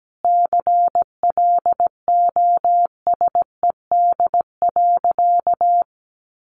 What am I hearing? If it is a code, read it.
Morse code, heard as a keyed tone, about 17 wpm, 700 Hz. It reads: CLOSED.
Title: Sentences from Top 1400 Words 17wpm